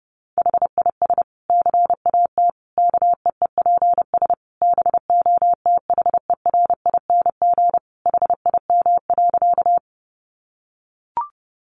Morse code, heard as a keyed tone, about 28 wpm, 700 Hz. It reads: HIS CAT KEEPS BOTHERING HIM.
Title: Additional Easy Phrases 30wpm